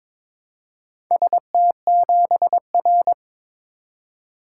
Morse code, heard as a keyed tone, about 21 wpm, 700 Hz. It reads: ST7R
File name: practice / 22wpm / Four Letter-Number Combinations - 22wpm